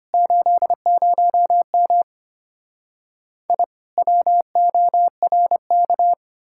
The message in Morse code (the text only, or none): 80M I WORK